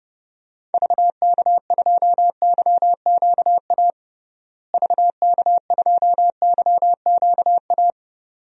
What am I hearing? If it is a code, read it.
Morse code, heard as a keyed tone, about 32 wpm, 700 Hz. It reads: VK2YQA VK2YQA
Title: DXCC-100 Call Signs - 6 Characters - Repeated 2x - 30wpm